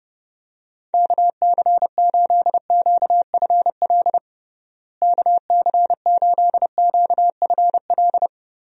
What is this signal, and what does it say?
Morse code, heard as a keyed tone, about 30 wpm, 700 Hz. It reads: KC8QFL KC8QFL
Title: USA Call Signs - 6 Characters - Repeated 3x - 30wpm